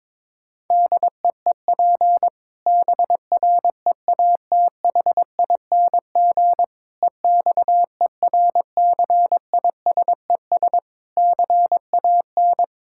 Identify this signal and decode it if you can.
Morse code, heard as a keyed tone, about 22 wpm, 700 Hz. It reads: DEEP BREATHING EXERCISES CAN
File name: Difficult Sentences with Hints - 22wpm